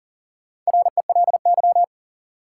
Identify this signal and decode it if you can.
Morse code, heard as a keyed tone, about 40 wpm, 700 Hz. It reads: RELY
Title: Sentences from Top 1200 Words - New Word Drill Alt - 40wpm